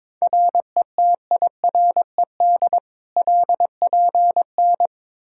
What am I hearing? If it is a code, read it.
Morse code, heard as a keyed tone, about 21 wpm, 700 Hz. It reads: RETIRED LPN
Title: Practice QSO - Exchange 3 - More Info - 22wpm